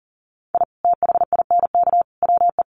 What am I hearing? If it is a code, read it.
Morse code, heard as a keyed tone, about 40 wpm, 700 Hz. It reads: I THINK WE
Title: Sets of 3 Words 40wpm